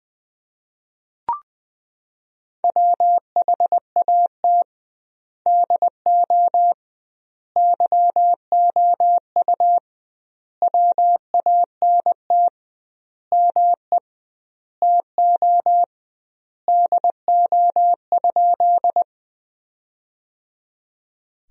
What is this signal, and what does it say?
Morse code, heard as a keyed tone, about 21 wpm, 700 Hz. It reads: WHAT DO YOU WANT ME TO DO?
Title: 2x Word-Spacing - Sentences from Top 200 Words - 20wpm